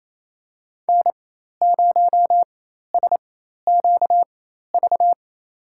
Morse code, heard as a keyed tone, about 28 wpm, spaced as for 13 wpm, 700 Hz. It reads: N0SQV
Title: USA Call Signs - 4x Character Spacing - 28wpm